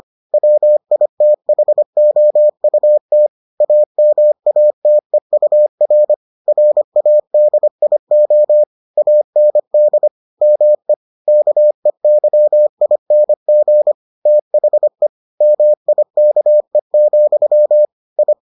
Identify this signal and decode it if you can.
Morse code, heard as a keyed tone, about 25 wpm, 600 Hz. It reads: WITHOUT AMATEUR RADIO AND ME KEYING THE MIKE, I